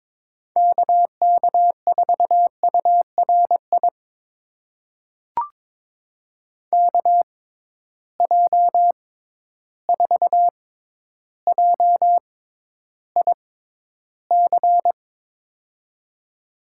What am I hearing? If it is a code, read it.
Morse code, heard as a keyed tone, about 22 wpm, spaced as for 11 wpm, 700 Hz. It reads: KK4URI KJ4JIC
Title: USA Call Signs - 6x Character Spacing - 22wpm